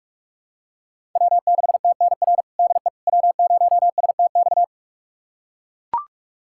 Morse code, heard as a keyed tone, about 43 wpm, 700 Hz. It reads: W6TNR DE W0STX
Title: USA Callsign DE Callsign - 5 Characters - 45wpm